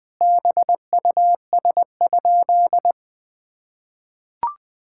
Morse code, heard as a keyed tone, about 21 wpm, 700 Hz. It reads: BUS?
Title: Sentences from Top 1000 Words - Review Part 6 - 20wpm